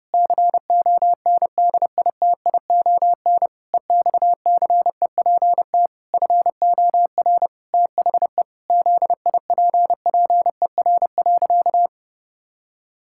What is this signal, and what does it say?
Morse code, heard as a keyed tone, about 28 wpm, 700 Hz. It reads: CONDITION EXCEPT FOR THE ZIPPER.
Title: Pangram Challenge v2 - 30wpm